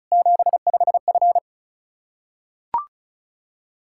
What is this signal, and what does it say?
Morse code, heard as a keyed tone, about 36 wpm, 700 Hz. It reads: 75F
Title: K1USN SST QSO Practice 35wpm